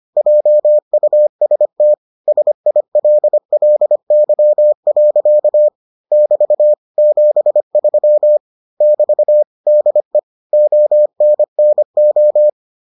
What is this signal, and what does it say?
Morse code, heard as a keyed tone, about 25 wpm, 600 Hz. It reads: JUST SILLY. <BT> 73 <BT> DE ONNO